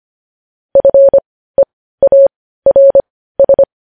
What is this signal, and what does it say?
Morse code, heard as a keyed tone, about 25 wpm, spaced as for 15 wpm, 550 Hz. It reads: FEARS
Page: news headlines in morse code at 15 wpm